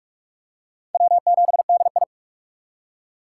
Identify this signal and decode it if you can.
Morse code, heard as a keyed tone, about 47 wpm, 700 Hz. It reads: W7DI
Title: USA Call Signs - 3x Character Spacing - 45wpm